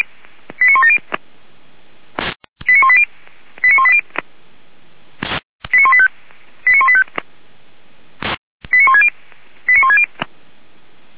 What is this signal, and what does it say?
Five-tone selective call (ZVEI), heard as quick runs of five tones, tones of about 70 ms. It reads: 89170 89180 89180 89185 89185 89160 89160